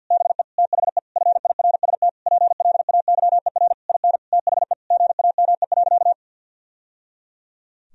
Morse code, heard as a keyed tone, about 50 wpm, 700 Hz. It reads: BE THE FIRST PLAYER IN THE GAME.
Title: Sentences from Top 1000 Words - Review Part 4 - 50wpm